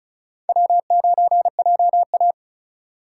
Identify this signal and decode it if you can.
Morse code, heard as a keyed tone, about 34 wpm, 700 Hz. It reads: W9JA
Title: USA Call Signs - 4 Characters - 4x Character Spacing - 35wpm